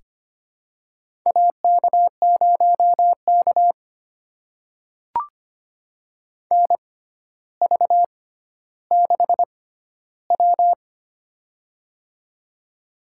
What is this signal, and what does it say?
Morse code, heard as a keyed tone, about 25 wpm, spaced as for 13 wpm, 700 Hz. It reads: AK0K NV6W
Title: USA Call Signs - 4 Characters - 6x Character Spacing - 25wpm